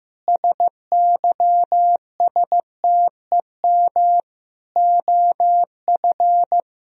Morse code, heard as a keyed tone, about 15 wpm, 700 Hz. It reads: SYSTEM OF
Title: Sentences from Top 1000 Words - Review Part 3 - 15wpm